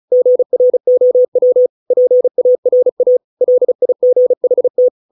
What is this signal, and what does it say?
Morse code, heard as a keyed tone, about 35 wpm, 500 Hz. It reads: GROW PARA LIGHT